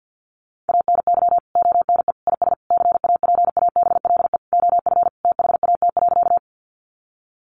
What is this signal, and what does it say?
Morse code, heard as a keyed tone, about 50 wpm, 700 Hz. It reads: ANY ONE IS CAPABLE OF THAT.
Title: Sentences from Top 1200 Words 50wpm